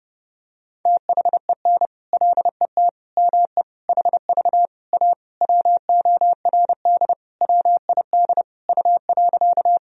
Morse code, heard as a keyed tone, about 30 wpm, 700 Hz. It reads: THEN LET ME HV A WORD WID U.
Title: Sentences from Top 300 Words and Abbreviations 30wpm